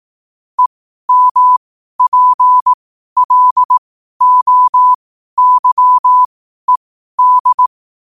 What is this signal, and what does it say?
Morse code, heard as a keyed tone, about 18 wpm, spaced as for 12 wpm, 1000 Hz. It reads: EMPLOYED
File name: 12wpm